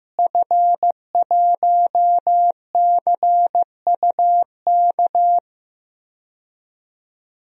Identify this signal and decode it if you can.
Morse code, heard as a keyed tone, about 15 wpm, 700 Hz. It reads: F1CUK
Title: DXCC-100 Call Signs - 5 Characters - Repeated 3x - 15wpm